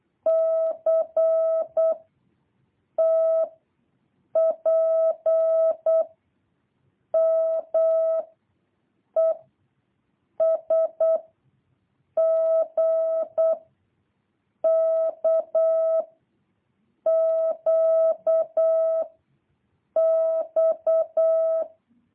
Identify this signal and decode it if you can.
Morse code, heard as a keyed tone, about 8 wpm, 650 Hz. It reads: C T P M E S G K Q X
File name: jh8cbh / 2023cw